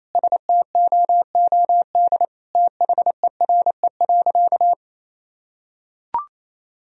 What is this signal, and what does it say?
Morse code, heard as a keyed tone, about 28 wpm, 700 Hz. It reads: STOOD THERE.
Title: Sentences from Top 1000 Words - Review Part 2 - 28wpm